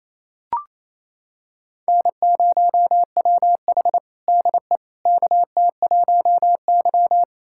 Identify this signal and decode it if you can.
Morse code, heard as a keyed tone, about 28 wpm, 700 Hz. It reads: N0WH DE KT1Y
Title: USA Callsign DE Callsign - 4 Characters - 28wpm